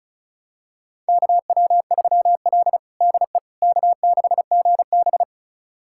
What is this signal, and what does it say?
Morse code, heard as a keyed tone, about 36 wpm, 700 Hz. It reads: KW3L DE K6GB
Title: USA Callsign DE Callsign - 4 Characters - 35wpm